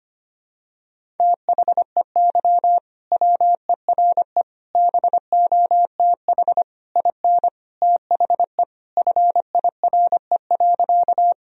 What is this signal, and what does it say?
Morse code, heard as a keyed tone, about 25 wpm, 700 Hz. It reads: THEY WERE BOTH IN THE FIRE.
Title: Sentences from Top 1000 Words - Review Part 1 - 25wpm